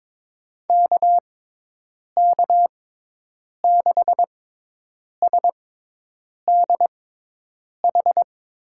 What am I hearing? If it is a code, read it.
Morse code, heard as a keyed tone, about 22 wpm, spaced as for 8 wpm, 700 Hz. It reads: KK6SDH
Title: USA Call Signs - 6 Characters - 6x Character Spacing - 22wpm